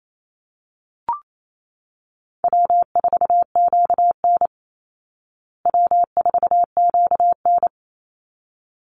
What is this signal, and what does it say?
Morse code, heard as a keyed tone, about 28 wpm, 700 Hz. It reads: W4QN W4QN
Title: Easy Call Signs Repeated 2x - 28wpm